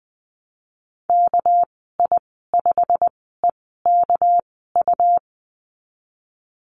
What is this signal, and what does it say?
Morse code, heard as a keyed tone, about 20 wpm, spaced as for 15 wpm, 700 Hz. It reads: KI5EKU
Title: 213 Lesson 34g - Number 8 - Call Signs 20wpm